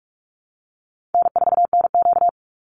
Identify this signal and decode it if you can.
Morse code, heard as a keyed tone, about 44 wpm, 700 Hz. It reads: N4NQ
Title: Easy Call Signs Repeated 2x - 45wpm